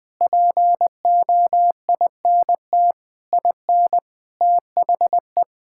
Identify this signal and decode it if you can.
Morse code, heard as a keyed tone, about 20 wpm, 700 Hz. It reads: POINT IN THE
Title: Sentences with 6-Word Phrases - Spoken Beginning - 20wpm